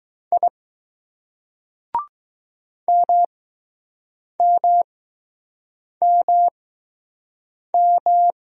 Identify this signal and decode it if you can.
Morse code, heard as a keyed tone, about 20 wpm, spaced as for 6 wpm, 700 Hz. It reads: I MMMM